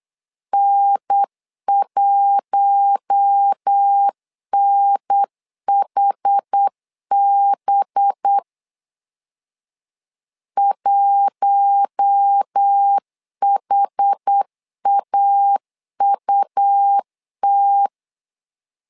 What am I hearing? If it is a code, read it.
Morse code, heard as a keyed tone, about 8 wpm, 800 Hz. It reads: N1NHB 1HAUT
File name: LEZIONI / 5 LBVC / 5 LB 8wpm 2816